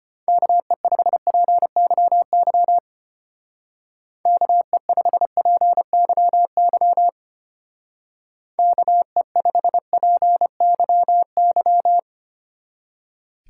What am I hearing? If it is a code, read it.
Morse code, heard as a keyed tone, about 30 wpm, 700 Hz. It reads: KE5PYY KE5PYY KE5PYY
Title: Speed Racing - USA 2x3 Call Signs 25wpm